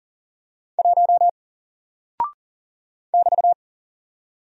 Morse code, heard as a keyed tone, about 39 wpm, 700 Hz. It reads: 1 <BT>
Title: Single Character (Letter:Number:Punctuation:Prosign) Alt 40wpm